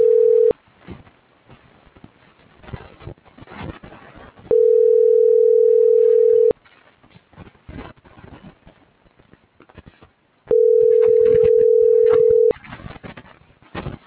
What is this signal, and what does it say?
Ringback tone (North America), 440 + 480 Hz.
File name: CENTRAL GENERAL / out tania.lemus